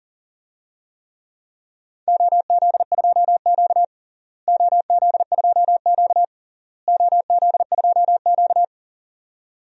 Morse code, heard as a keyed tone, about 37 wpm, 700 Hz. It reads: OZ2Q OZ2Q OZ2Q
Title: DXCC-100 Call Signs - 4 Characters - Repeated 3x - 40wpm